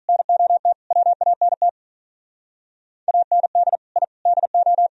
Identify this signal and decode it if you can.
Morse code, heard as a keyed tone, about 44 wpm, 700 Hz. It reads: NOT WANT AND I DO
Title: Speed-Racing - Sets of 5 Words 35wpm